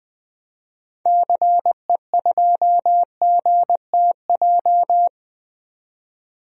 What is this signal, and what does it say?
Morse code, heard as a keyed tone, about 20 wpm, 700 Hz. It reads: CE2GTJ
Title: DXCC-100 Call Signs - 6 Characters - 20wpm